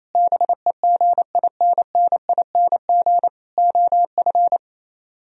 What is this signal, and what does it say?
Morse code, heard as a keyed tone, about 34 wpm, 700 Hz. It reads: BEGINNING OF
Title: Speed-Racing - Sets of 4 Words 28wpm